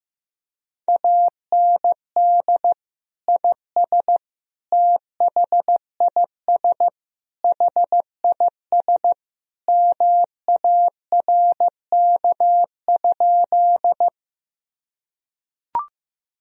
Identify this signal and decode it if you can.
Morse code, heard as a keyed tone, about 15 wpm, 700 Hz. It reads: AND IS THIS HIS MARK?
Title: Sentences from Top 1000 Words - Review Part 1 - 15wpm